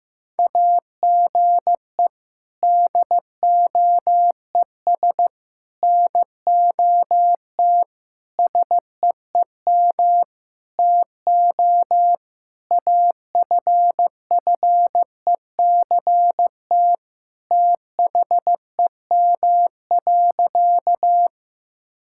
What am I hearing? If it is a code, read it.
Morse code, heard as a keyed tone, about 15 wpm, 700 Hz. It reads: AGE DOES NOT SEEM TO AFFECT THEM.
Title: Sentences from Top 1000 Words - Review Part 6 - 15wpm